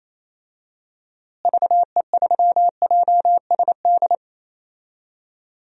Morse code, heard as a keyed tone, about 28 wpm, 700 Hz. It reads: VE3JSD
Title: DXCC-100 Call Signs - 6 Characters - Repeated 3x - 28wpm